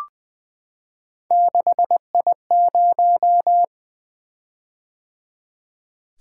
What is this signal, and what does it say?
Morse code, heard as a keyed tone, about 20 wpm, 700 Hz. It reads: 6I0
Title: Three Letter-Number Combinations - 20wpm